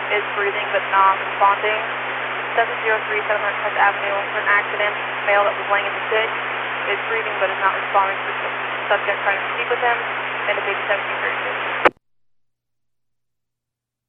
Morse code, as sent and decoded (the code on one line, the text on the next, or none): none